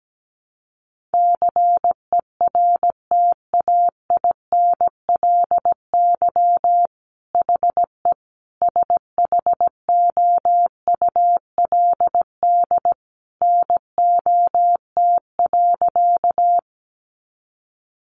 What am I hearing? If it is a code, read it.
Morse code, heard as a keyed tone, about 17 wpm, 700 Hz. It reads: CERTAINLY HE SHOULD NOT.
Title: Sentences from Top 900 Words 17wpm